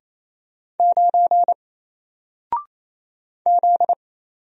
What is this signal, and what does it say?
Morse code, heard as a keyed tone, about 28 wpm, 700 Hz. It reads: 9 Z
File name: Single-Letter-Number-28wpm